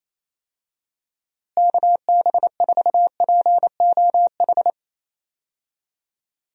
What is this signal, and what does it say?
Morse code, heard as a keyed tone, about 27 wpm, 700 Hz. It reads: KB4POH
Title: USA Call Signs - 6 Characters - Repeated 3x - 28wpm